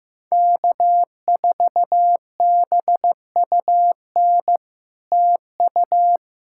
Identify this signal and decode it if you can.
Morse code, heard as a keyed tone, about 15 wpm, 700 Hz. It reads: K4BUN TU
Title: SOTA QSO Practice 15wpm